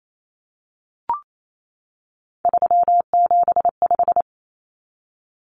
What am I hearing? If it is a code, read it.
Morse code, heard as a keyed tone, about 28 wpm, 700 Hz. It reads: 375
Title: Common Cut Numbers - 3-Digit 28wpm